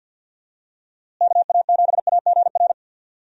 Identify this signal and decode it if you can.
Morse code, heard as a keyed tone, about 52 wpm, 700 Hz. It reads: KA7AGR
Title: USA Call Signs - 6 Characters - Repeated 2x - 50wpm